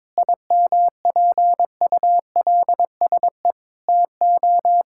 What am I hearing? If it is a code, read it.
Morse code, heard as a keyed tone, about 22 wpm, 700 Hz. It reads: IMPULSE TO
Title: Pangram Challenge v2 - 22wpm